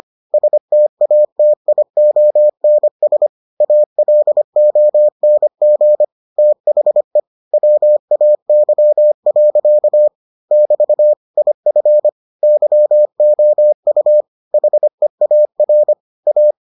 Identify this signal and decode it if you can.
Morse code, heard as a keyed tone, about 25 wpm, 600 Hz. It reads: STATIONS ALONG THE WAY. <BT> IF YOU HEAR A